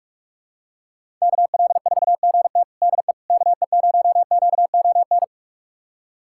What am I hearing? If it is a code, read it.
Morse code, heard as a keyed tone, about 42 wpm, 700 Hz. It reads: KL4GT DE KE0QON
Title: USA Callsign DE Callsign 45wpm